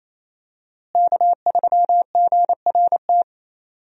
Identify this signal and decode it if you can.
Morse code, heard as a keyed tone, about 28 wpm, 700 Hz. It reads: K3GRT
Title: USA Call Signs - 5 Characters - Repeated 3x - 28wpm